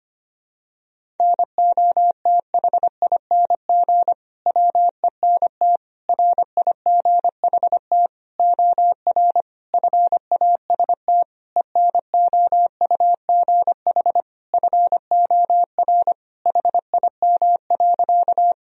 Morse code, heard as a keyed tone, about 25 wpm, 700 Hz. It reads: NOTHING WENT RIGHT OR FAST ENOUGH FOR HIM.
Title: Sentences from Top 1000 Words - Review Part 2 - 25wpm